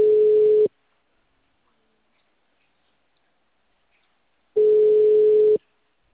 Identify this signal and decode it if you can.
Ringback tone (Europe, 425 Hz).